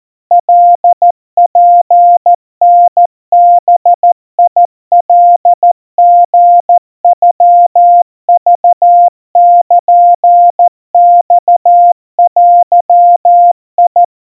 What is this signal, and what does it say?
Morse code (Japanese Wabun code), heard as a keyed tone, about 14 wpm, 700 Hz. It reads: カツタバカリノクルマデ
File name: kamyu 50 cpm 700 Hz Crispy